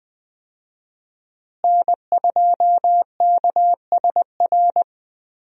Morse code, heard as a keyed tone, about 21 wpm, 700 Hz. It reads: N2KSR
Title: USA Call Signs - 5 Characters - Repeated 3x - 20wpm